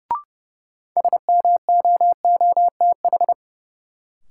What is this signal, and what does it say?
Morse code, heard as a keyed tone, about 30 wpm, 700 Hz. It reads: SMOOTH
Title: Sentences from Top 1400 Words - New Word Drill Alt - 30wpm